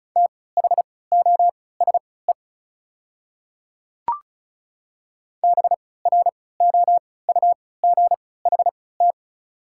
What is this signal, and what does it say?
Morse code, heard as a keyed tone, about 36 wpm, spaced as for 20 wpm, 690 Hz. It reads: THOSE BROUGHT
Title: 3x Character Spacing - Top 500 Words - 35wpm